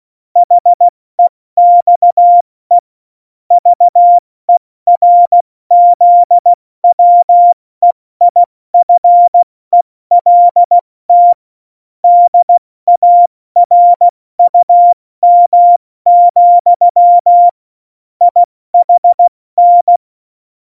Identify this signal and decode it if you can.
Morse code, heard as a keyed tone, about 16 wpm, 700 Hz. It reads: HEXE VERZWEIFELT DARUM, IHN
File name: Morsekurs / Witze / Witz 30. Juli 2019, 70 F80